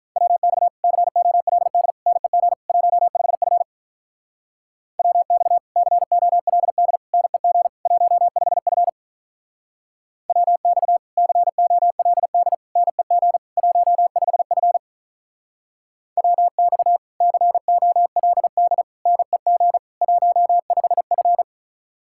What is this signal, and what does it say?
Morse code, heard as a keyed tone, about 47 wpm, 700 Hz. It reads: WX COLD NEG 15F WX COLD NEG 15F WX COLD NEG 15F WX COLD NEG 15F